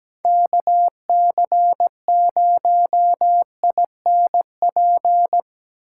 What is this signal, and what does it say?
Morse code, heard as a keyed tone, about 17 wpm, 700 Hz. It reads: KC0INP
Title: K1USN SST QSO Practice - 17wpm